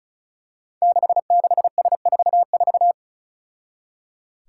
Morse code, heard as a keyed tone, about 36 wpm, 700 Hz. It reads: 66S44
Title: Five Challenging Letter-Number Combinations 35wpm